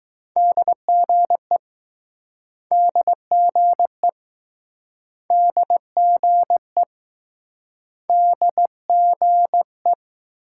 Morse code, heard as a keyed tone, about 20 wpm, 700 Hz. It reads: DGE DGE DGE DGE